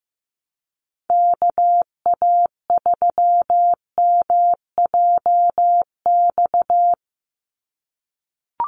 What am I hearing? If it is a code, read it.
Morse code, heard as a keyed tone, about 15 wpm, 700 Hz. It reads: KA3MJX
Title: USA Call Signs 15wpm